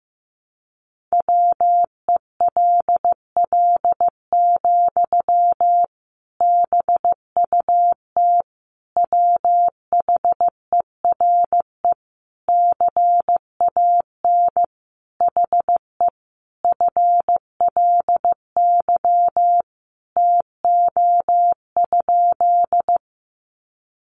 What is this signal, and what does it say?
Morse code, heard as a keyed tone, about 15 wpm, 700 Hz. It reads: WELL, BUT WHERE CAN HE FLY TO?
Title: Sentences from Top 500 Words 15wpm